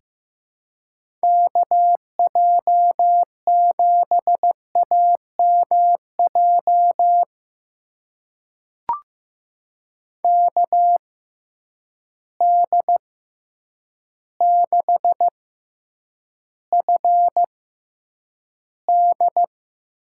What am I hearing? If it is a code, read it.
Morse code, heard as a keyed tone, about 15 wpm, spaced as for 8 wpm, 700 Hz. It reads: KJ7AMJ KD6FD
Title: USA Call Signs - 6 Characters - 6x Character Spacing - 15wpm